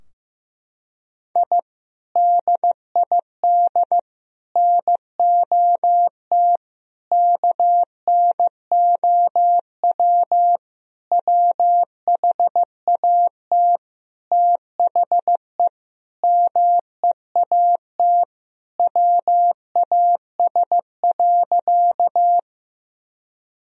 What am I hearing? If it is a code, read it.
Morse code, heard as a keyed tone, about 15 wpm, 700 Hz. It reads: I DID NOT KNOW WHAT THE MEAT WAS.
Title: Sentences from Top 1000 Words - Review Part 3 - 15wpm